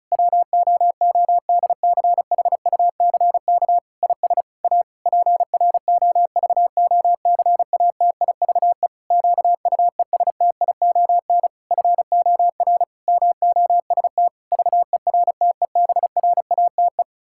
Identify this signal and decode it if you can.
Morse code, heard as a keyed tone, about 34 wpm, 700 Hz. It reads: WOODCHUCK IS A PROVOCATIVE QUESTION FOR MOST VERTEBRATE
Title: Pangram Challenge v2 - 35wpm